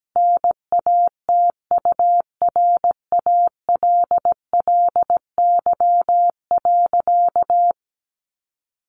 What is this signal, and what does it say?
Morse code, heard as a keyed tone, about 17 wpm, 700 Hz. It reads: NATURALLY.
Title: Sentences from Top 1500 Words 17wpm